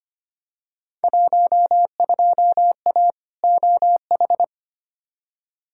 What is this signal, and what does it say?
Morse code, heard as a keyed tone, about 25 wpm, 700 Hz. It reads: 12A OH
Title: Field Day Exchanges Encore - 25wpm